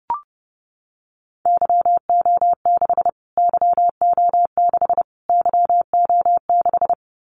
Morse code, heard as a keyed tone, about 30 wpm, 700 Hz. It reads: YO6 YO6 YO6
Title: Callsign Prefixes for Top 25 DXCC - Repeated 3x - 30wpm